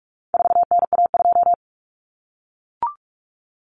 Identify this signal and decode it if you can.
Morse code, heard as a keyed tone, about 44 wpm, 700 Hz. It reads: 4NA2
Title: Common Cut Numbers - 4-Digit 45wpm